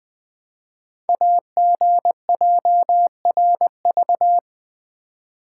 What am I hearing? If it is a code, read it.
Morse code, heard as a keyed tone, about 20 wpm, 700 Hz. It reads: AGJRV
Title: Code Groups - 5 Letter:Numbers - 20wpm